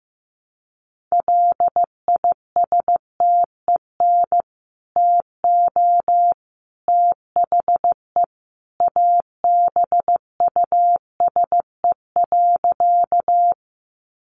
Morse code, heard as a keyed tone, about 15 wpm, 700 Hz. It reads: LISTEN TO THE ABUSE.
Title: Sentences from Top 1300 Words 15wpm